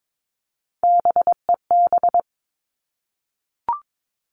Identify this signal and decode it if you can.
Morse code, heard as a keyed tone, about 22 wpm, 700 Hz. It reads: BEB
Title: Three Challenging Letter-Number Combinations 22wpm